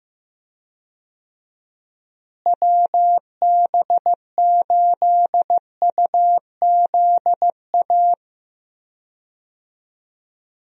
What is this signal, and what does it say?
Morse code, heard as a keyed tone, about 15 wpm, 700 Hz. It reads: WB8UZA
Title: USA Call Signs - 6 Characters - Repeated 2x - 15wpm